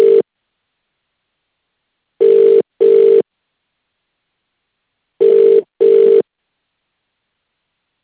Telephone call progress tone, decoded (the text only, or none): ringback tone